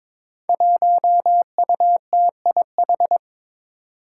Morse code, heard as a keyed tone, about 22 wpm, 700 Hz. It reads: 1UTIH
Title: Five Challenging Letter-Number Combinations 22wpm